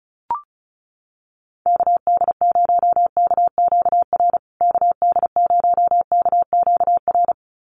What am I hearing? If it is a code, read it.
Morse code, heard as a keyed tone, about 34 wpm, 700 Hz. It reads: KD0KQR KD0KQR
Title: USA Call Signs Repeated 2x 35wpm